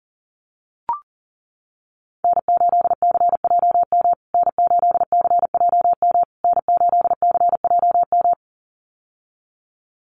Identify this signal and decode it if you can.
Morse code, heard as a keyed tone, about 40 wpm, 700 Hz. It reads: N8CJM N8CJM N8CJM
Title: USA Call Signs Repeated 3x 40wpm